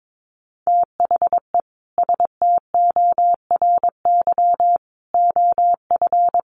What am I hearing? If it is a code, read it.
Morse code, heard as a keyed tone, about 22 wpm, 700 Hz. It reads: THE STORY OF
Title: Sets of 3 Words 22wpm